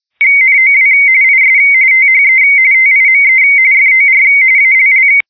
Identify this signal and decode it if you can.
RTTY, about 45.45 baud. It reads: NOW IS THE TIME FOR ALL
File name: digimodes / Digimodes mp3 / RTTY-45-170